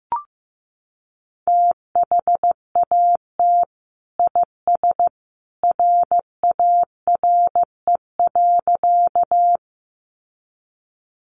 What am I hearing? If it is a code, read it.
Morse code, heard as a keyed tone, about 15 wpm, 700 Hz. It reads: THAT IS RARE.
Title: Sentences from Top 1100 Words 15wpm